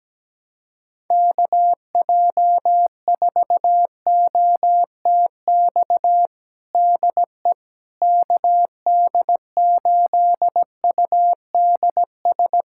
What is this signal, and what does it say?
Morse code, heard as a keyed tone, about 17 wpm, 700 Hz. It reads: KJ4OTX DE KD8UDS
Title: USA Callsign DE Callsign - 6 Characters - 17wpm